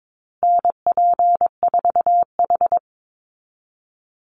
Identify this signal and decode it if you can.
Morse code, heard as a keyed tone, about 22 wpm, 700 Hz. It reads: NP4H
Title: Call Signs Repeated 2x - 22wpm